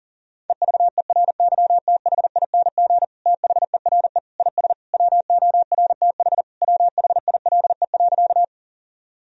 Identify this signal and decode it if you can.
Morse code, heard as a keyed tone, about 37 wpm, 700 Hz. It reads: EVERYTHING THERE IS WORTH WHILE.
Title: Sentences from Top 1000 Words - Review Part 1 - 40wpm